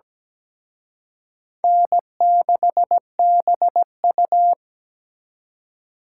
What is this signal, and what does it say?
Morse code, heard as a keyed tone, about 20 wpm, 700 Hz. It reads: N6BU
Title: Speed Racing - USA 1x2 Call Signs 17wpm